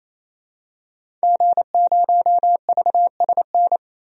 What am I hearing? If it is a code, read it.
Morse code, heard as a keyed tone, about 28 wpm, 700 Hz. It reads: G0VSN
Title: DXCC-100 Call Signs - 5 Characters - 28wpm